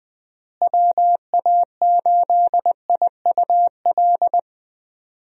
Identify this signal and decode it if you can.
Morse code, heard as a keyed tone, about 21 wpm, 700 Hz. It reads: WA8IUL
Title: USA Call Signs - 6 Characters - Repeated 3x - 20wpm